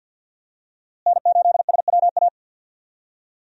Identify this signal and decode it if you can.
Morse code, heard as a keyed tone, about 48 wpm, 700 Hz. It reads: N8SWA
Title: USA Call Signs - 5 Characters - Repeated 2x - 50wpm